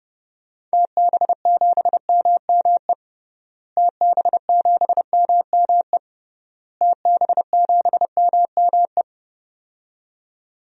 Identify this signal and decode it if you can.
Morse code, heard as a keyed tone, about 32 wpm, 700 Hz. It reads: TB7MME TB7MME TB7MME
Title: DXCC-100 Call Signs - 6 Characters - Repeated 3x - 30wpm